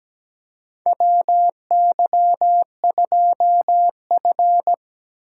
Morse code, heard as a keyed tone, about 17 wpm, 700 Hz. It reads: WY2F
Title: USA Call Signs - 4 Characters - 4x Character Spacing - 17wpm